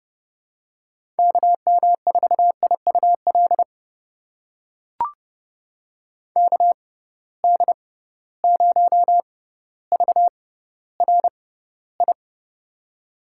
Morse code, heard as a keyed tone, about 32 wpm, spaced as for 16 wpm, 700 Hz. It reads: KM4IUL KD0VRI
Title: USA Call Signs - 6 Characters - 6x Character Spacing - 30wpm